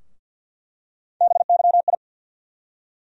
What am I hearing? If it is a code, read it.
Morse code, heard as a keyed tone, about 47 wpm, 700 Hz. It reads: BYI
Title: USA Call Sign Postfixes - 3 Letters 50wpm